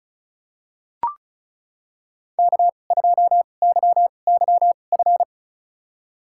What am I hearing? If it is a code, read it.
Morse code, heard as a keyed tone, about 36 wpm, spaced as for 25 wpm, 700 Hz. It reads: K2YYF
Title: USA Call Signs - 5 Characters - 2x Character Spacing - 35wpm